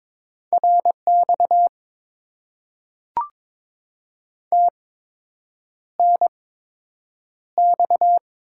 Morse code, heard as a keyed tone, about 22 wpm, spaced as for 8 wpm, 690 Hz. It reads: RX TNX
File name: Top 90 QSO Elements - 8x Character Spacing - 22wpm